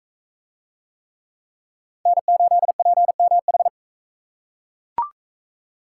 Morse code, heard as a keyed tone, about 40 wpm, 700 Hz. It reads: N8PMH